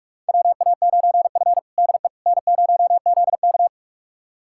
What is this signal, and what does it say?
Morse code, heard as a keyed tone, about 42 wpm, 700 Hz. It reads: WA9F DE N0ZK
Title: USA Callsign DE Callsign - 4 Characters - 45wpm